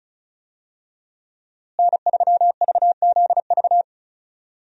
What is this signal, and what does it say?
Morse code, heard as a keyed tone, about 36 wpm, 700 Hz. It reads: N3VZV
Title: USA Call Signs - 5 Characters - Repeated 2x - 35wpm